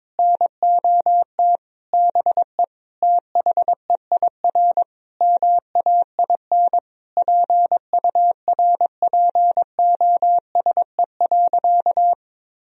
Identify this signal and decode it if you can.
Morse code, heard as a keyed tone, about 22 wpm, 700 Hz. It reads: NOT BE THEIR MAIN PURPOSE.
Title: Sentences from Top 1000 Words - Review Part 2 - 22wpm